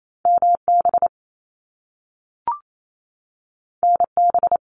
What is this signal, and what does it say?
Morse code, heard as a keyed tone, about 28 wpm, 700 Hz. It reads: MB NB
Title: Canadian Provinces and Territories 28wpm